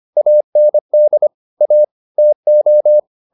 Morse code, heard as a keyed tone, about 25 wpm, 600 Hz. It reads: AND A TO